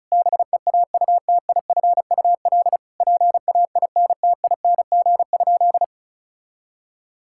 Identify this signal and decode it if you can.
Morse code, heard as a keyed tone, about 36 wpm, 700 Hz. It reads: BEAUTIFUL PAINTING?
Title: Sentences from Top 1000 Words - Review Part 6 - 35wpm